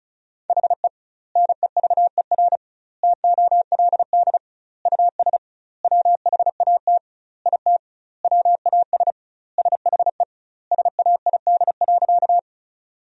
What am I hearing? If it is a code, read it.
Morse code, heard as a keyed tone, about 34 wpm, 700 Hz. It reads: HE NEVER TOLD US WHAT IT WAS SHE SAID.
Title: 2x Word-Spacing - Sentences from Top 400 Words - 35wpm